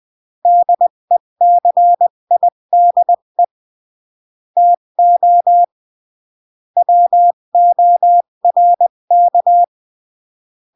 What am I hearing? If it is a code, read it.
Morse code, heard as a keyed tone, about 20 wpm, 700 Hz. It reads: DECIDE TO WORK
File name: CW / jodorowsky / 82 commandments 20-15